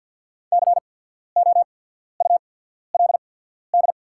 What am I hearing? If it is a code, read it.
Morse code, heard as a keyed tone, about 48 wpm, spaced as for 14 wpm, 700 Hz. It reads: COULD
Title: 8x Character Spacing - Top 300 Words - 50wpm